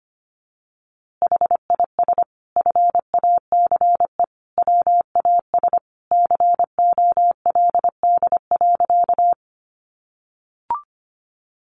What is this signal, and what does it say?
Morse code, heard as a keyed tone, about 25 wpm, 700 Hz. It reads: HIS FACE WAS COLD.
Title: A Sound Of Thunder 25wpm